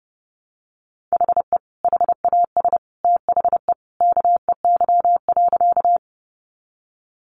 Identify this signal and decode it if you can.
Morse code, heard as a keyed tone, about 30 wpm, 700 Hz. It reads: HE HAS THE KEY.
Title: Sentences from Top 600 Words 30wpm